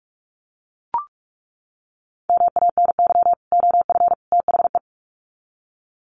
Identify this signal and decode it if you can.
Morse code, heard as a keyed tone, about 44 wpm, 700 Hz. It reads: MANY OF THE
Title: Sets of 3 Words 45wpm